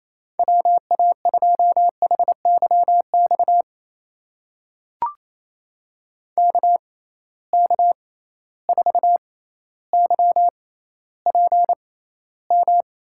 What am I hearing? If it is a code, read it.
Morse code, heard as a keyed tone, about 28 wpm, spaced as for 14 wpm, 700 Hz. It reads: WA2HYX KK4YPM
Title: USA Call Signs - 6 Characters - 6x Character Spacing - 28wpm